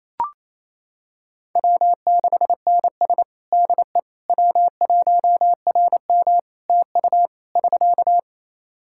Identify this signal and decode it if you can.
Morse code, heard as a keyed tone, about 27 wpm, 700 Hz. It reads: W6NS DE W1RM TU <SK>
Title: K1USN SST QSO Practice 28wpm